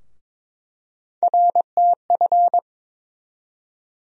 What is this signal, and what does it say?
Morse code, heard as a keyed tone, about 22 wpm, 700 Hz. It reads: RTF
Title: USA Call Sign Postfixes - 3 Letters 22wpm